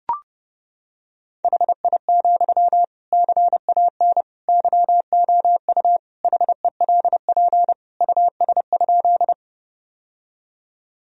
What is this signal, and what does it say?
Morse code, heard as a keyed tone, about 29 wpm, 700 Hz. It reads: HI, CAN YOU HELP US?
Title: Sentences from Top 1000 Words - Review Part 3 - 30wpm